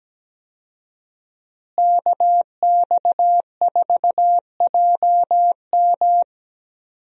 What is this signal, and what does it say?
Morse code, heard as a keyed tone, about 17 wpm, 700 Hz. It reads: KX4JM
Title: USA Call Signs - 5 Characters - Repeated 3x - 17wpm